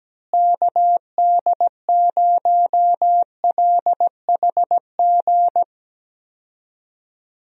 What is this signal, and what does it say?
Morse code, heard as a keyed tone, about 17 wpm, 700 Hz. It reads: KD0LHG
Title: USA Call Signs - 6 Characters - Repeated 2x - 17wpm